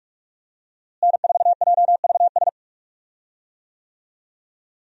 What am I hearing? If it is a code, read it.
Morse code, heard as a keyed tone, about 46 wpm, 700 Hz. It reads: N4JVS
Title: USA Call Signs - 5 Characters - Repeated 3x - 45wpm